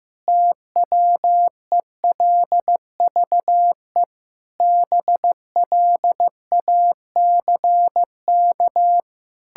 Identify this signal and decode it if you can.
Morse code, heard as a keyed tone, about 15 wpm, 700 Hz. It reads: TWELVE BLACK
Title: Pangram Challenge v2 - 15wpm